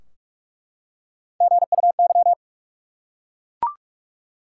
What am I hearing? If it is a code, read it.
Morse code, heard as a keyed tone, about 41 wpm, 700 Hz. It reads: GUY
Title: Three Letter Words - 45wpm